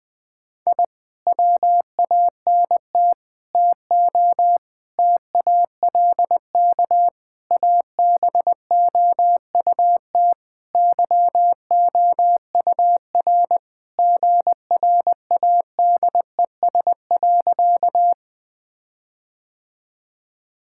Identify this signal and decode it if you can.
Morse code, heard as a keyed tone, about 20 wpm, 700 Hz. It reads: I WANT TO TALK ABOUT YOUR GRADES.
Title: Sentences with 4-Word Phrases - 20wpm